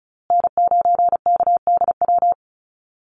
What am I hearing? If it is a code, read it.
Morse code, heard as a keyed tone, about 34 wpm, 700 Hz. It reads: N9KDW
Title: USA Call Signs Repeated 3x 35wpm